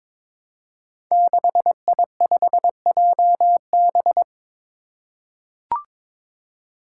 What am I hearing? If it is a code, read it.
Morse code, heard as a keyed tone, about 22 wpm, 700 Hz. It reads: 6I5JB
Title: Five Challenging Letter-Number Combinations 22wpm